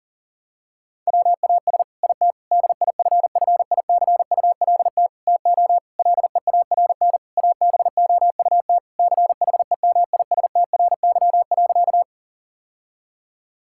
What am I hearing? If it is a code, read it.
Morse code, heard as a keyed tone, about 40 wpm, 700 Hz. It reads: WAS IT DIFFICULT TO LEARN ABOUT CHEMISTRY.
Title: Sentences from Top 1000 Words - Review Part 6 - 40wpm